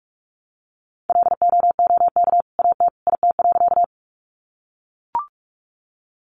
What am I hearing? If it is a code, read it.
Morse code, heard as a keyed tone, about 44 wpm, 700 Hz. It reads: LOOK AT IT.
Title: Sentences from Top 100 Words 45wpm